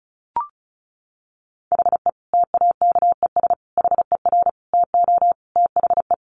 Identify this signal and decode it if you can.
Morse code, heard as a keyed tone, about 34 wpm, 700 Hz. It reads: HE TAKES HER TO THE
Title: Easy Phrases 35wpm